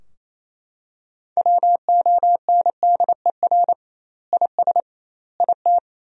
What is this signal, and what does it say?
Morse code, heard as a keyed tone, about 28 wpm, 700 Hz. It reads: WONDER IS IT
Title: 2x Word-Spacing - Sentences from Top 500 Words - 28wpm